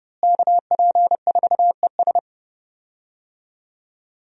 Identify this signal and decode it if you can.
Morse code, heard as a keyed tone, about 30 wpm, 700 Hz. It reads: KP4ES
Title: USA Call Signs - 5 Characters - Repeated 3x - 30wpm